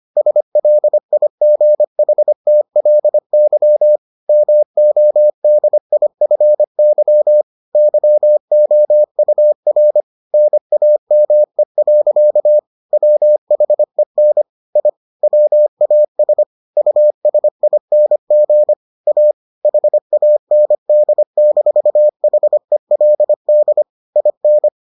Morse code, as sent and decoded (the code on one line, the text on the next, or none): ... .-.. .. --. .... - .-.. -.-- / -- --- -.. .. ..-. -.-- / -.-- --- ..- .-. / -. .- -- . .-.-.- / .-- .... -. / .. / .-- .- ... / ..- ... .. -. --. / .- / .... .- -. -.. -....- .... .-.. -.. / .. -.
SLIGHTLY MODIFY YOUR NAME. WHEN I WAS USING A HAND-HELD IN